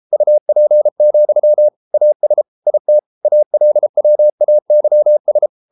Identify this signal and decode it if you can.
Morse code, heard as a keyed tone, about 33 wpm, 600 Hz. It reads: UP, AS IT ALWAYS